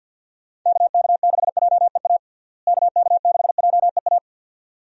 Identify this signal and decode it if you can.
Morse code, heard as a keyed tone, about 48 wpm, 700 Hz. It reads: KK6JEA KK6JEA
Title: USA Call Signs - 6 Characters - Repeated 2x - 50wpm